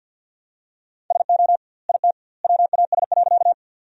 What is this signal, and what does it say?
Morse code, heard as a keyed tone, about 50 wpm, 700 Hz. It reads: SO IT WAS.
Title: 2x Word-Spacing - Sentences from Top 100 Words - 50wpm